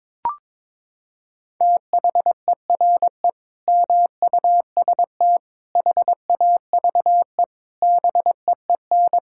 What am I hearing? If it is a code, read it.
Morse code, heard as a keyed tone, about 22 wpm, 700 Hz. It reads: THERE MUST HAVE BEEN
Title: Sentences from Top 1000 Words - Review Part 6 - 22wpm